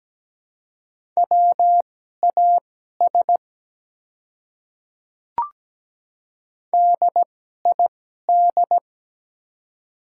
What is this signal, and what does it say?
Morse code, heard as a keyed tone, about 17 wpm, spaced as for 12 wpm, 700 Hz. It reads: WAS DID
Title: 2x Character Spacing - Top 200 Words - 17wpm